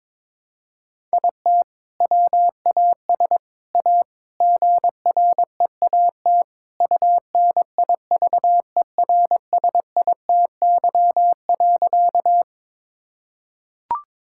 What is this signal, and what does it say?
Morse code, heard as a keyed tone, about 22 wpm, 700 Hz. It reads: IT WAS A GREAT UNIVERSITY.
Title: Sentences from Top 1000 Words - Review Part 5 - 22wpm